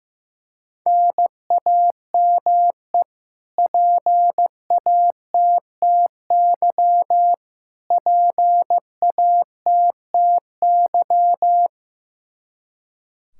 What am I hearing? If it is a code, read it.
Morse code, heard as a keyed tone, about 16 wpm, 700 Hz. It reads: NAME PATTY PATTY
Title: Practice QSO - Exchange 1 - Beginning - 15wpm